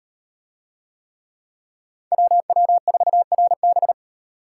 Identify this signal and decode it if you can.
Morse code, heard as a keyed tone, about 38 wpm, 700 Hz. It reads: WW4RB